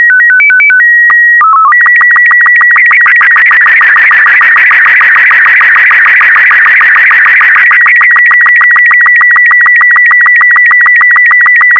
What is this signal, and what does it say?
SSTV (slow-scan television), Robot 36.